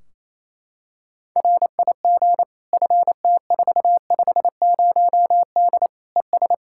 Morse code, heard as a keyed tone, about 28 wpm, 700 Hz. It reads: RIG FT450D ES
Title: Practice QSO - Exchange 2 - Info - 28wpm